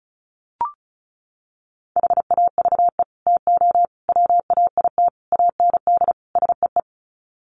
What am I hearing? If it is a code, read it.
Morse code, heard as a keyed tone, about 34 wpm, 700 Hz. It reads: HAVE TO WAIT AND SEE
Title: Sets of 5 Words Encore2 35wpm